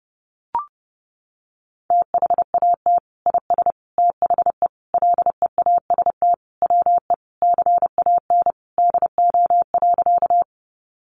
Morse code, heard as a keyed tone, about 30 wpm, 700 Hz. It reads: THAT IS THE LEAST WE CAN DO.
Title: Sentences from Top 700 Words 30wpm